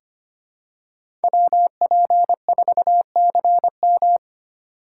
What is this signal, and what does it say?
Morse code, heard as a keyed tone, about 25 wpm, 700 Hz. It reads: WP4CM
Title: USA Call Signs - 5 Characters - 3x Character Spacing - 25wpm